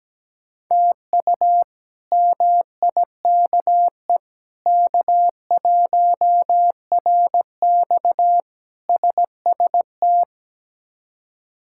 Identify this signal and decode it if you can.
Morse code, heard as a keyed tone, about 17 wpm, 700 Hz. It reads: TU MIKE K1RX SST
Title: K1USN SST QSO Practice - 17wpm